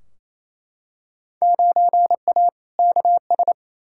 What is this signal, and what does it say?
Morse code, heard as a keyed tone, about 28 wpm, 700 Hz. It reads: 9A KS
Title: Field Day Exchanges Encore - 28wpm